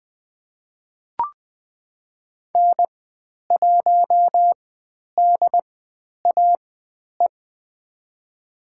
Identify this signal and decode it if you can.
Morse code, heard as a keyed tone, about 20 wpm, spaced as for 10 wpm, 700 Hz. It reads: N1DAE
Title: 095 Lesson 16g - Letter c - Call Signs 20wpm